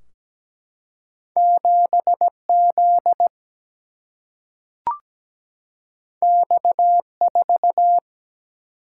Morse code, heard as a keyed tone, about 17 wpm, 700 Hz. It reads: 7Z X4
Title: Single-Letter Single-Number 17wpm